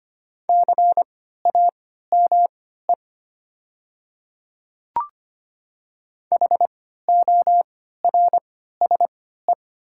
Morse code, heard as a keyed tone, about 25 wpm, spaced as for 14 wpm, 700 Hz. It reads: CAME HORSE
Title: 3x Character Spacing - Top 300 Words - 25wpm